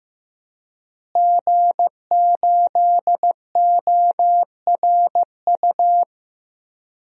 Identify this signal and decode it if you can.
Morse code, heard as a keyed tone, about 15 wpm, 700 Hz. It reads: G8ORU
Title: DXCC-100 Call Signs - 5 Characters - Repeated 2x - 15wpm